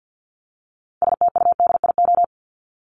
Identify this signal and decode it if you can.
Morse code, heard as a keyed tone, about 50 wpm, 700 Hz. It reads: STUDIO
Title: Top 1001-1250 Words 50wpm